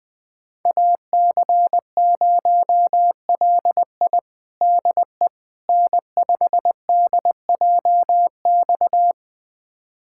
Morse code, heard as a keyed tone, about 21 wpm, 700 Hz. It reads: AC0LI DE N5DJX
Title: USA Callsign DE Callsign 20wpm